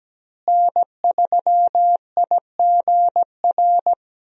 Morse code, heard as a keyed tone, about 17 wpm, 700 Hz. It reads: N3IGR
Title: USA Call Signs - 5 Characters - Repeated 3x - 17wpm